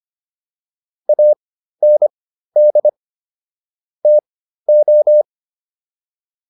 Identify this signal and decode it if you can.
Morse code, heard as a keyed tone, about 25 wpm, spaced as for 13 wpm, 600 Hz. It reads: AND TO